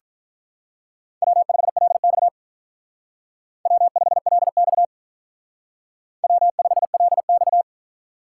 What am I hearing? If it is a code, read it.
Morse code, heard as a keyed tone, about 46 wpm, 700 Hz. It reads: W5LX W5LX W5LX